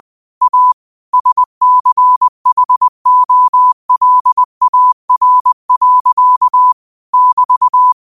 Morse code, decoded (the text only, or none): A SCHOLAR. <BT>